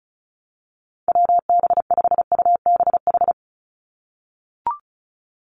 Morse code, heard as a keyed tone, about 34 wpm, 700 Hz. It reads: WB5UBH
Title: USA Call Signs 35wpm